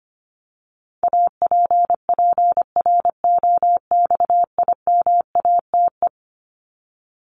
Morse code, heard as a keyed tone, about 25 wpm, 700 Hz. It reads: APPROXIMATE
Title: LY Suffix - Top 100 Words with Base Words 25wpm